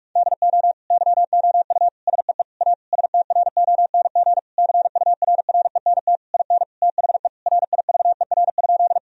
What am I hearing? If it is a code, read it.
Morse code, heard as a keyed tone, about 46 wpm, 700 Hz. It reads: DO YOU SEE A STRONG CURRENT IN THE RIVER?
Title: Sentences from Top 1000 Words - Review Part 4 - 45wpm